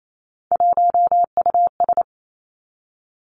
Morse code, heard as a keyed tone, about 28 wpm, 700 Hz. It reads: 1US
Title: Three Challenging Letter-Number Combinations 28wpm